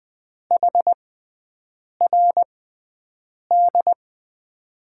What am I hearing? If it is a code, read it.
Morse code, heard as a keyed tone, about 20 wpm, spaced as for 7 wpm, 700 Hz. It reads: HRD
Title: USA Call Signs - 6x Character Spacing - 20wpm